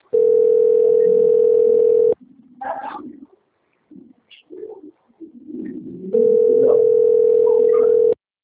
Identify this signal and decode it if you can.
Ringback tone (North America), 440 + 480 Hz.
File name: out- - - - - .40833